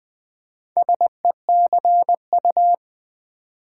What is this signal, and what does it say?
Morse code, heard as a keyed tone, about 20 wpm, 700 Hz. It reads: SECU
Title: Build A Word - 2 Letters At A Time - 8-Letter Words - 20wpm